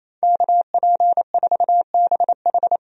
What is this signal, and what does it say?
Morse code, heard as a keyed tone, about 28 wpm, 700 Hz. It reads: KP4BH
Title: DXCC-100 Call Signs - 5 Characters - 28wpm